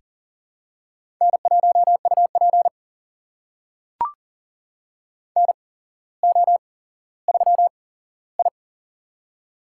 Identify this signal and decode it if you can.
Morse code, heard as a keyed tone, about 37 wpm, spaced as for 17 wpm, 700 Hz. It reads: N1UP NO3I
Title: USA Call Signs - 4 Characters - 8x Character Spacing - 40wpm